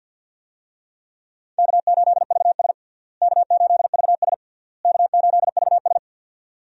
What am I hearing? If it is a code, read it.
Morse code, heard as a keyed tone, about 46 wpm, 700 Hz. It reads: K8VS K8VS K8VS
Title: USA Call Signs - 4 Characters - Repeated 3x - 50wpm